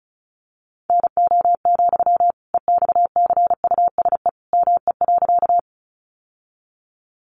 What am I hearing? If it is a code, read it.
Morse code, heard as a keyed tone, about 34 wpm, 700 Hz. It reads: NO, EXCUSE ME.
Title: Sentences from Top 700 Words 35wpm